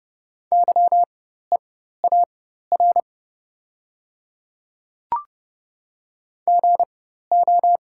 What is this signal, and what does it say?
Morse code, heard as a keyed tone, about 30 wpm, spaced as for 14 wpm, 700 Hz. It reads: YEAR GO
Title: 4x Character Spacing - Top 200 Words - 30wpm